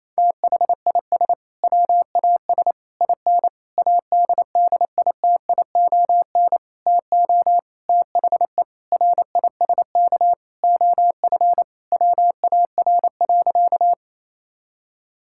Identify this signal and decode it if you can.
Morse code, heard as a keyed tone, about 28 wpm, 700 Hz. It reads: THIS WAS IN ADDITION TO THE RISK OF WAR.
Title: Sentences from Top 1000 Words - Review Part 6 - 28wpm